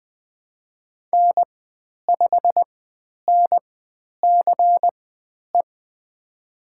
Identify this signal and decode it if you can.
Morse code, heard as a keyed tone, about 20 wpm, spaced as for 10 wpm, 700 Hz. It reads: N5NCE
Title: 095 Lesson 16g - Letter c - Call Signs 20wpm